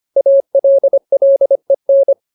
Morse code, heard as a keyed tone, about 25 wpm, 550 Hz. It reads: ALLEN
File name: Ultimate Word List / 25WPM Ultimate